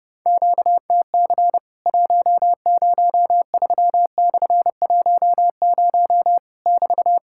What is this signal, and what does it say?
Morse code, heard as a keyed tone, about 30 wpm, 700 Hz. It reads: QTC 103/10 <BT>
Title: Worked All Europe DX Contest - QTC Exchange - 30wpm